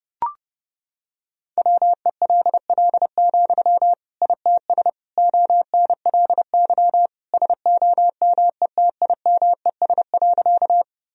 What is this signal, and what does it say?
Morse code, heard as a keyed tone, about 30 wpm, 700 Hz. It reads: WELL, ITS ONLY SOMETIMES.
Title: Sentences from Top 1000 Words - Review Part 3 - 30wpm